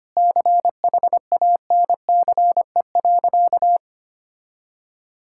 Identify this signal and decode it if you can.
Morse code, heard as a keyed tone, about 25 wpm, 700 Hz. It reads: CHANCE.
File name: practice / 25wpm / Sentences with 3-Word Phrases - 25wpm